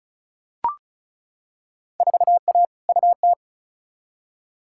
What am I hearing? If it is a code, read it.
Morse code, heard as a keyed tone, about 34 wpm, 700 Hz. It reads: 4A UT
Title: Field Day Exchanges - 35wpm